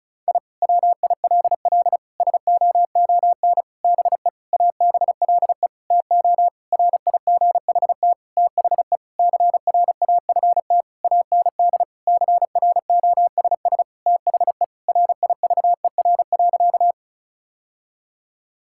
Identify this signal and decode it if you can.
Morse code, heard as a keyed tone, about 36 wpm, 700 Hz. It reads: I WILL SOON BE ABLE TO RIGHT THE CRAFT AND CROSS THE RIVER.
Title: Sentences from Top 1000 Words - Review Part 6 - 35wpm